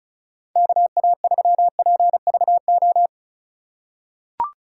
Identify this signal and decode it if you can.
Morse code, heard as a keyed tone, about 36 wpm, 700 Hz. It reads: KA3PVO
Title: USA Call Signs - 6 Characters - 3x Character Spacing - 35wpm